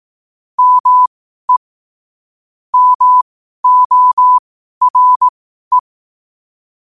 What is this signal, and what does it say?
Morse code, heard as a keyed tone, about 18 wpm, spaced as for 12 wpm, 1000 Hz. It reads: ME MORE
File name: 2026-01-07-12wpm